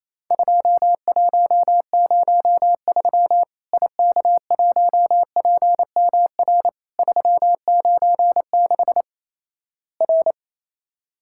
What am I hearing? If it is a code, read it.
Morse code, heard as a keyed tone, about 28 wpm, 700 Hz. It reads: 2103 IK1PMR 396 R
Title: Worked All Europe DX Contest - QTC Exchange - 28wpm